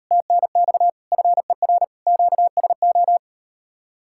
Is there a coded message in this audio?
Yes: Morse code from 0.1 to 3.2 s, heard as a keyed tone, about 38 wpm, 700 Hz. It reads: TNX FER QSO